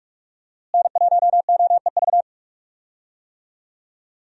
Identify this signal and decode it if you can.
Morse code, heard as a keyed tone, about 42 wpm, 700 Hz. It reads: N1OEV
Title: USA Call Signs - 5 Characters - Repeated 3x - 45wpm